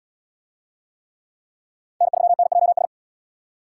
Morse code, heard as a keyed tone, about 74 wpm, 700 Hz. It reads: N3NJS